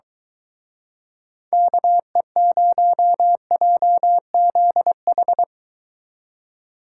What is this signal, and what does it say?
Morse code, heard as a keyed tone, about 23 wpm, 700 Hz. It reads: KE0JZH